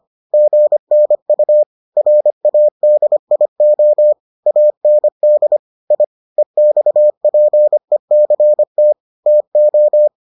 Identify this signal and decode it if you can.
Morse code, heard as a keyed tone, about 25 wpm, 600 Hz. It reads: GNU RADIO AND I EXPECT TO